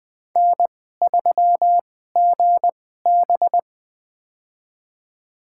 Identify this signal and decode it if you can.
Morse code, heard as a keyed tone, about 20 wpm, spaced as for 14 wpm, 700 Hz. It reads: N3GB
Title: USA Call Signs - 4 Characters - 2x Character Spacing - 20wpm